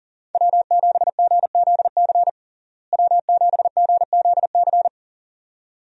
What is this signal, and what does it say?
Morse code, heard as a keyed tone, about 40 wpm, 700 Hz. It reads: W7GZC W7GZC
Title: USA Call Signs - 5 Characters - Repeated 2x - 40wpm